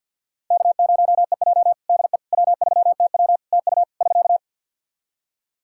Morse code, heard as a keyed tone, about 50 wpm, 700 Hz. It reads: K0EJ DE W3TW TU <SK>
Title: Practice QSO - Exchange 4 - Ending - 50wpm